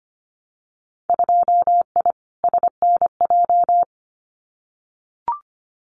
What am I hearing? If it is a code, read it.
Morse code, heard as a keyed tone, about 25 wpm, 700 Hz. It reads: 2I SNJ
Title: Winter Field Day Exchanges Encore 25wpm